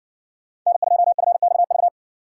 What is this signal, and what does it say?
Morse code, heard as a keyed tone, about 60 wpm, 700 Hz. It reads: N2FXV